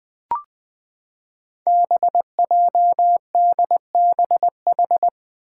Morse code, heard as a keyed tone, about 20 wpm, 700 Hz. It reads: BJDBH
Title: Five Challenging Letter-Number Combinations 20wpm